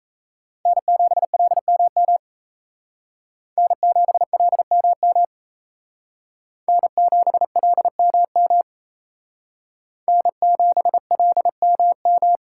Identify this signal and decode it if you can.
Morse code, heard as a keyed tone, about 35 wpm, 700 Hz. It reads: N7LMM N7LMM N7LMM N7LMM